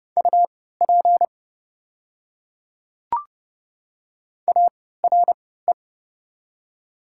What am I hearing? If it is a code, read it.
Morse code, heard as a keyed tone, about 30 wpm, spaced as for 17 wpm, 700 Hz. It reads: UP ARE
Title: 3x Character Spacing - Top 100 Words - 30wpm